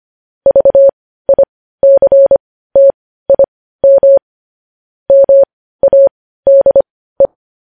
Morse code, heard as a keyed tone, about 25 wpm, spaced as for 15 wpm, 550 Hz. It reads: VICTIM MADE
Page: News Headlines in Morse Code at 15 WPM